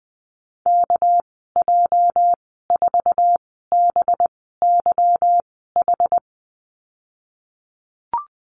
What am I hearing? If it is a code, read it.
Morse code, heard as a keyed tone, about 20 wpm, spaced as for 15 wpm, 700 Hz. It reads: KJ4BYH
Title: 238 Lesson 38g - Letter q - Call Signs 20wpm